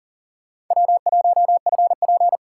Morse code, heard as a keyed tone, about 40 wpm, 700 Hz. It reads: W1FP
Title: USA Call Signs - 4 Characters - 2x Character Spacing - 40wpm